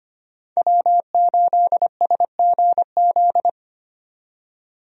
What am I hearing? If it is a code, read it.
Morse code, heard as a keyed tone, about 25 wpm, 700 Hz. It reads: W8SGZ
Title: USA Call Signs - 5 Characters - 25wpm